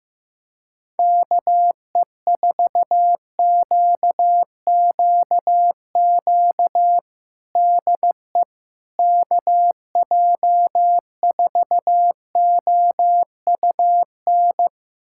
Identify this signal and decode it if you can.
Morse code, heard as a keyed tone, about 15 wpm, 700 Hz. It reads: KE4QQQ DE KJ4OUN
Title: USA Callsign DE Callsign - 6 Characters - 15wpm